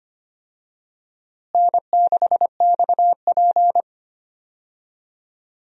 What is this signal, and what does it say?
Morse code, heard as a keyed tone, about 25 wpm, 700 Hz. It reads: N6XP
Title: USA Call Signs - 4 Characters - Repeated 2x - 25wpm